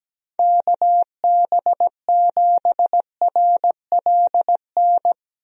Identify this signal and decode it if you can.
Morse code, heard as a keyed tone, about 17 wpm, 700 Hz. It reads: KB7RLN
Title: USA Call Signs - 6 Characters - Repeated 2x - 17wpm